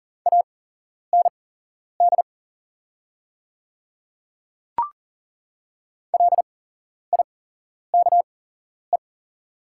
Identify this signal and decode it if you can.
Morse code, heard as a keyed tone, about 40 wpm, spaced as for 11 wpm, 700 Hz. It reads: AND LIKE
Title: 8x Character Spacing - Top 300 Words - 40wpm